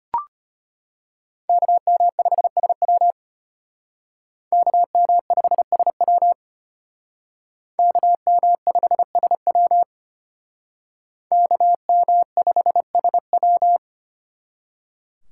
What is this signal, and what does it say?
Morse code, heard as a keyed tone, about 33 wpm, 700 Hz. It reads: KM5SW KM5SW KM5SW KM5SW